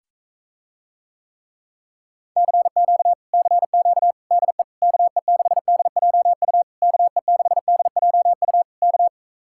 Morse code, heard as a keyed tone, about 44 wpm, 700 Hz. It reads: CQ CQ DE KE6DJU KE6DJU K